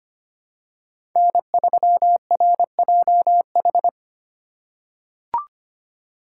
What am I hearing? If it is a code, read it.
Morse code, heard as a keyed tone, about 25 wpm, 700 Hz. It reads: N3RJH
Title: USA Call Signs - 5 Characters - 3x Character Spacing - 25wpm